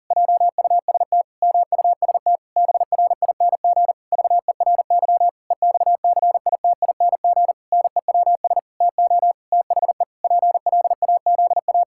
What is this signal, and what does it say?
Morse code, heard as a keyed tone, about 40 wpm, 700 Hz. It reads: JUST MUST BRING VERY EXCITING NEWS TO THE PLAZA
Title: Pangram Challenge v2 - 40wpm